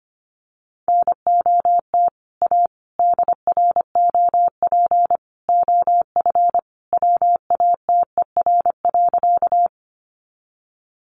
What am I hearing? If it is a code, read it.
Morse code, heard as a keyed tone, about 25 wpm, 700 Hz. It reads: NOT A DROP OF WATER.
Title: Sentences from Top 1400 Words 25wpm